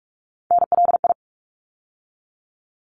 Morse code, heard as a keyed tone, about 46 wpm, 700 Hz. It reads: NLI
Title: ARRL Sections 45wpm